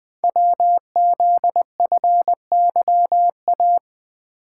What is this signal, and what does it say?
Morse code, heard as a keyed tone, about 20 wpm, 700 Hz. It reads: WZFYA
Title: Code Groups - 5 Letters - 20wpm